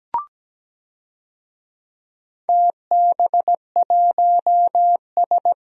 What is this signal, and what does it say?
Morse code, heard as a keyed tone, about 17 wpm, 700 Hz. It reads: TB1S
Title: DXCC-100 Call Signs - 4 Characters - Repeated 3x - 17wpm